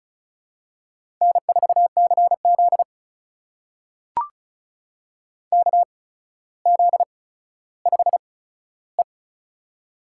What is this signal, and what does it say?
Morse code, heard as a keyed tone, about 36 wpm, spaced as for 15 wpm, 700 Hz. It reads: N4CZ KZ5E
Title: USA Call Signs - 8x Character Spacing - 35wpm